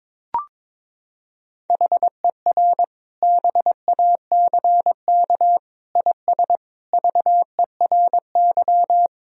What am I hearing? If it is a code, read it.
Morse code, heard as a keyed tone, about 22 wpm, 700 Hz. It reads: HER BACK IS VERY
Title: Sentences from Top 1000 Words - Review Part 1 - 22wpm